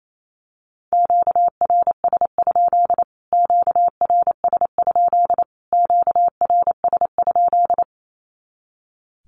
Morse code, heard as a keyed tone, about 28 wpm, 700 Hz. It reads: QRS? QRS? QRS?
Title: Q-Codes Repeated 3x 28wpm